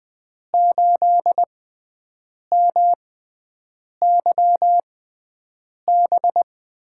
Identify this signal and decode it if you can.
Morse code, heard as a keyed tone, about 20 wpm, spaced as for 7 wpm, 700 Hz. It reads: 8MYB
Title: USA Call Signs - 6x Character Spacing - 20wpm